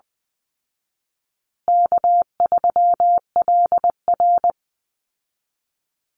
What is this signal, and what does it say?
Morse code, heard as a keyed tone, about 20 wpm, 700 Hz. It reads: K3LR
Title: Call Signs Repeated 2x - 20wpm